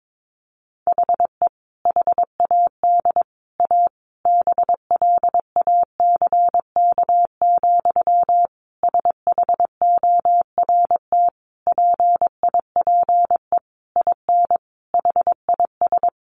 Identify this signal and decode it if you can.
Morse code, heard as a keyed tone, about 22 wpm, 700 Hz. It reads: HE HAD A BLACK, SHORT PIPE IN HIS
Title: Sentences from Top 1500 Words 22wpm